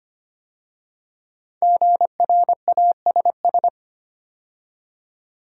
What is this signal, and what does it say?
Morse code, heard as a keyed tone, about 25 wpm, 700 Hz. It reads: GRASS
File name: 5 Letter Words - First 3 Letters Given - 25wpm